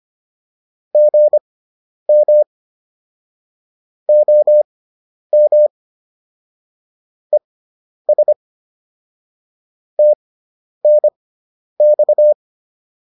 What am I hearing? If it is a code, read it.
Morse code, heard as a keyed tone, about 25 wpm, spaced as for 10 wpm, 600 Hz. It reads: GM OM ES TNX
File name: QSO208 10